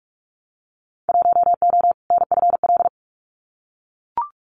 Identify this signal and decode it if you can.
Morse code, heard as a keyed tone, about 44 wpm, 700 Hz. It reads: 1O NFL
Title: Winter Field Day Exchanges 45wpm